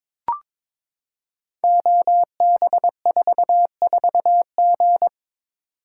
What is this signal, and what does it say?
Morse code, heard as a keyed tone, about 23 wpm, 700 Hz. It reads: OB44G
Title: Code Groups - 5 Letter:Numbers - 22wpm